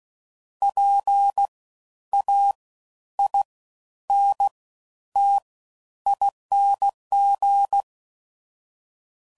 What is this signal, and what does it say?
Morse code, heard as a keyed tone, about 16 wpm, spaced as for 10 wpm, 800 Hz. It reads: PAINTING
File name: Suffixes / Special INGs / Fast Ings